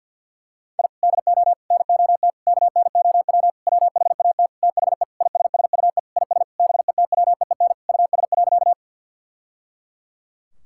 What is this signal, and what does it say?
Morse code, heard as a keyed tone, about 53 wpm, 700 Hz. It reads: I DO NOT KNOW WHAT THE ISSUE IS BETWEEN US.
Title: Sentences from Top 1000 Words - Review Part 6 - 50wpm